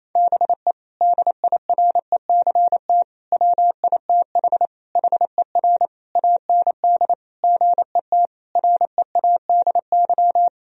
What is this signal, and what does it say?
Morse code, heard as a keyed tone, about 28 wpm, 700 Hz. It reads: BE DIRECT WITH HER AND GET READY
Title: Sentences from Top 1000 Words - Review Part 2 - 28wpm